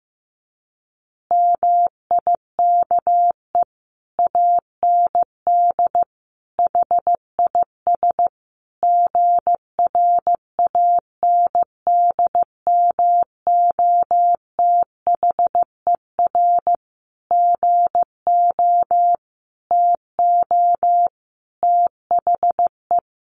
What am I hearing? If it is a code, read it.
Morse code, heard as a keyed tone, about 15 wpm, 700 Hz. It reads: MIKE AND HIS GRANDMOTHER GO TO THE
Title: Additional Easy Phrases 15wpm